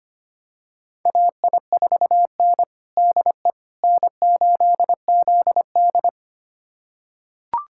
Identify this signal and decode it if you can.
Morse code, heard as a keyed tone, about 25 wpm, 700 Hz. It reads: AI4N DE N8ZD
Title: USA Callsign DE Callsign - 4 Characters - 25wpm